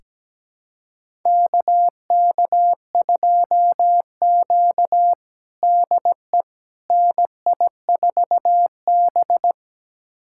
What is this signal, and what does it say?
Morse code, heard as a keyed tone, about 17 wpm, 700 Hz. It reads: KK2Q DE NI4B
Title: USA Callsign DE Callsign - 4 Characters - 17wpm